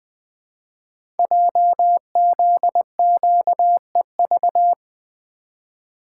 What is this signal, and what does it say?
Morse code, heard as a keyed tone, about 20 wpm, 700 Hz. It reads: JZQEV
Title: Code Groups - 5 Letters - 20wpm